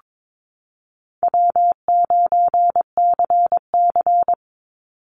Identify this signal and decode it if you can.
Morse code, heard as a keyed tone, about 22 wpm, 700 Hz. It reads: W9CC
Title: Easy Call Signs Repeated 2x - 22wpm